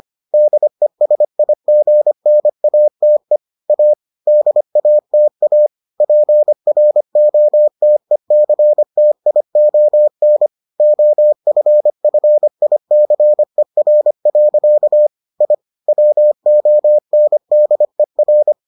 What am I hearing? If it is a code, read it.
Morse code, heard as a keyed tone, about 25 wpm, 600 Hz. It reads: DESIGNATE A DATA PROTECTION OFFICER. I WONDER